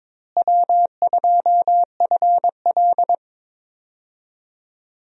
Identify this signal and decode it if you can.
Morse code, heard as a keyed tone, about 22 wpm, 700 Hz. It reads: W2FL
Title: USA Call Signs - 4 Characters - Repeated 2x - 22wpm